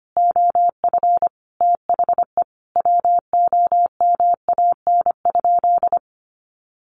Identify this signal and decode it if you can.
Morse code, heard as a keyed tone, about 25 wpm, 700 Hz. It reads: OF THE WOMAN?
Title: Sentences from Top 1400 Words 25wpm